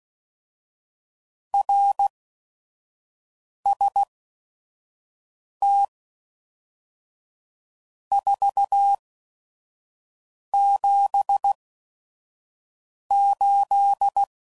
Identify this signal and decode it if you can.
Morse code, heard as a keyed tone, about 16 wpm, spaced as for 5 wpm, 800 Hz. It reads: RST 478
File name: CW Code Groups @5WPM / RST's / Your RST's